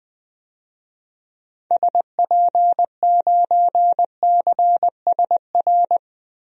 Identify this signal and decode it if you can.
Morse code, heard as a keyed tone, about 20 wpm, 700 Hz. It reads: SP9CSR
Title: DXCC-100 Call Signs - 6 Characters - 20wpm